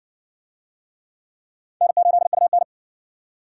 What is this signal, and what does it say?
Morse code, heard as a keyed tone, about 54 wpm, 700 Hz. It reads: N8UN